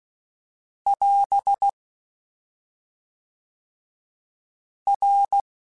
Morse code, heard as a keyed tone, about 16 wpm, 800 Hz. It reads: <AS> R